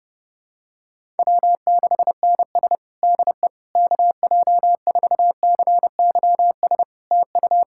Morse code, heard as a keyed tone, about 30 wpm, 700 Hz. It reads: W6NS DE KJ4CYS TU
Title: Practice QSO - Exchange 4 - Ending - 30wpm